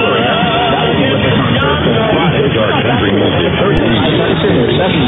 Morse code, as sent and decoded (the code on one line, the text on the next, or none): none